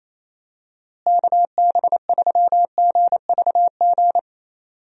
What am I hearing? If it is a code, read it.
Morse code, heard as a keyed tone, about 28 wpm, 700 Hz. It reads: KB3GVG
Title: USA Call Signs - 6 Characters - Repeated 3x - 28wpm